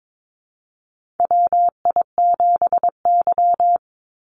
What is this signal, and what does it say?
Morse code, heard as a keyed tone, about 22 wpm, 700 Hz. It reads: WI7Y
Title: USA Call Signs 22wpm